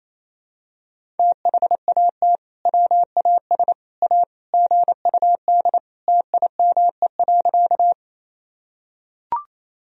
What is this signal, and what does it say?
Morse code, heard as a keyed tone, about 27 wpm, 700 Hz. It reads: THAT WAS A GUD TIME.
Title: Sentences from Top 200 Words and Abbreviations 28wpm